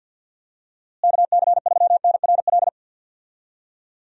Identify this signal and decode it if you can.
Morse code, heard as a keyed tone, about 52 wpm, 700 Hz. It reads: KC3NRL
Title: USA Call Signs - 6 Characters - Repeated 3x - 50wpm